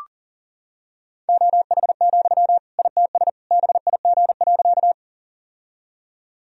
Morse code, heard as a keyed tone, about 40 wpm, 700 Hz. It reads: OH, ITS BIG.
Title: Sentences from Top 1000 Words - Review Part 1 - 40wpm